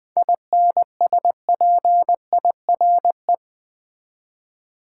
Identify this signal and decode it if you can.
Morse code, heard as a keyed tone, about 20 wpm, 700 Hz. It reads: INSPIRE
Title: 7 Letter Words - First 3 Letters Given - 20wpm